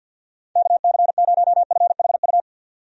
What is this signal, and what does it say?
Morse code, heard as a keyed tone, about 51 wpm, 700 Hz. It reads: KC0FHU
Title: USA Call Signs - 6 Characters - 50wpm